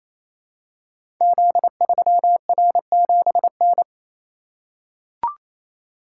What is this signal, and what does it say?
Morse code, heard as a keyed tone, about 29 wpm, 700 Hz. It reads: Z3R7N
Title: Code Groups - 5 Letter:Numbers - 28wpm